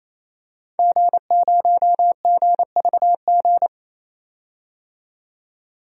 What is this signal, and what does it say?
Morse code, heard as a keyed tone, about 28 wpm, 700 Hz. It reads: G0GVG
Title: DXCC-100 Call Signs - 5 Characters - 28wpm